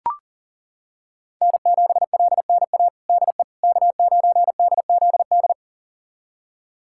Morse code, heard as a keyed tone, about 40 wpm, 700 Hz. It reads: N7LNA DE K9DZD
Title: USA Callsign DE Callsign - 5 Characters - 40wpm